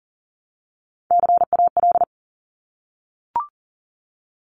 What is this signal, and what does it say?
Morse code, heard as a keyed tone, about 38 wpm, 700 Hz. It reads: CAL
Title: Top Three Letter Combinations in Words 40wpm